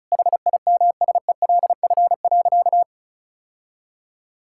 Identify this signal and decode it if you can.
Morse code, heard as a keyed tone, about 36 wpm, 700 Hz. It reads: HIMSELF.
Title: Sentences from Top 1600 Words 35wpm